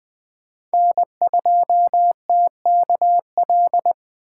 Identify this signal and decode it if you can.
Morse code, heard as a keyed tone, about 20 wpm, 700 Hz. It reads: N2TKL
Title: USA Call Signs - 5 Characters - 4x Character Spacing - 20wpm